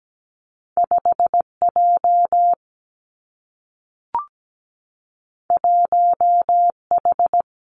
Morse code, heard as a keyed tone, about 17 wpm, 700 Hz. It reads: 5 J 1 H
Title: Two Challenging Letter-Number Combinations 17wpm